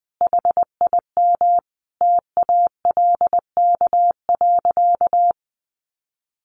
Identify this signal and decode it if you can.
Morse code, heard as a keyed tone, about 20 wpm, 700 Hz. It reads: HIM TALK.
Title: Sentences from Top 600 Words 20wpm